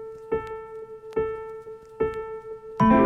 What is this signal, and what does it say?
There is no code.